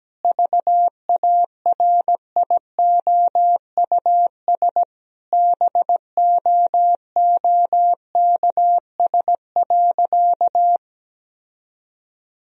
Morse code, heard as a keyed tone, about 17 wpm, 700 Hz. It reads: VARIOUS BOOKS.
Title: Sentences from Top 1000 Words - Review Part 4 - 17wpm